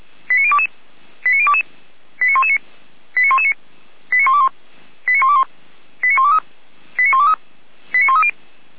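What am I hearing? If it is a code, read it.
Five-tone selective call (ZVEI), heard as quick runs of five tones, tones of about 70 ms. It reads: 89022 89022 89119 89119 89121 89121 89123 89123 89130
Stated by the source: ZVEI five-tone selective call received by SDR.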